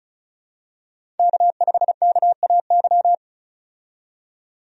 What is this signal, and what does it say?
Morse code, heard as a keyed tone, about 36 wpm, 700 Hz. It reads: K5KAY
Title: USA Call Signs - 5 Characters - 35wpm